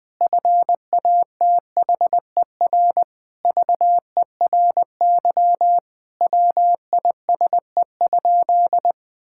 Morse code, heard as a keyed tone, about 20 wpm, 700 Hz. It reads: FATHER VERY WISE?
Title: Sentences from Top 1600 Words 20wpm